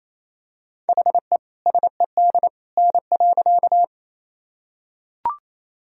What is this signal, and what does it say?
Morse code, heard as a keyed tone, about 28 wpm, 700 Hz. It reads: HE SED N.
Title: Sentences from Top 100 Words and Abbreviations 28wpm